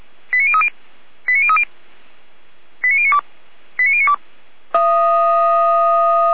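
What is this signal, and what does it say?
Five-tone selective call (ZVEI), heard as quick runs of five tones, tones of about 70 ms. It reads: 89030 89030 89092 89092
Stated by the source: ZVEI five-tone selective call received on an SDR.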